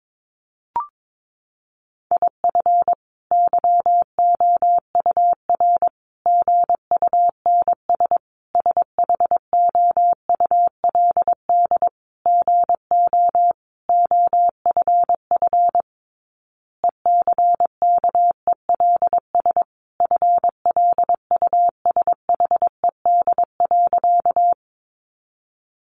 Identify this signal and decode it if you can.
Morse code, heard as a keyed tone, about 22 wpm, 700 Hz. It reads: IF YOUR GUNS SHOULD GO OFF ECKELS FLUSHED.
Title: A Sound Of Thunder 22wpm